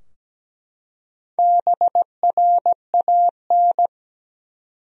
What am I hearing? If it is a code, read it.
Morse code, heard as a keyed tone, about 17 wpm, 700 Hz. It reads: BRAN
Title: USA Common Names - 17wpm